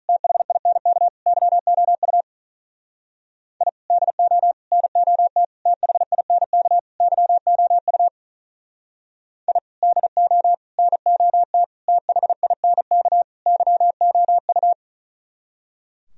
Morse code, heard as a keyed tone, about 46 wpm, 700 Hz. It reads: THINK YOU I DO NOT THINK YOU I DO NOT THINK YOU